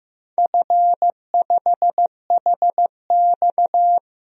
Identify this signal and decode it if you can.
Morse code, heard as a keyed tone, about 15 wpm, 700 Hz. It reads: F5HX
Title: DXCC-100 Call Signs - 4 Characters - 15wpm